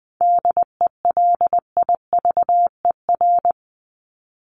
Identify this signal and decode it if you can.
Morse code, heard as a keyed tone, about 20 wpm, 700 Hz. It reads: DELIVER
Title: Top 1751-2000 Words 20wpm